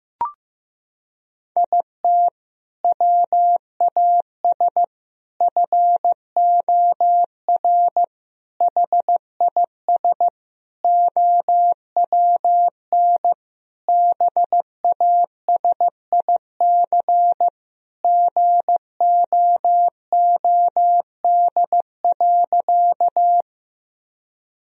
Morse code, heard as a keyed tone, about 15 wpm, 700 Hz. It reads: IT WAS FOR HIS OWN BASIC GOOD.
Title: Sentences from Top 1000 Words - Review Part 3 - 15wpm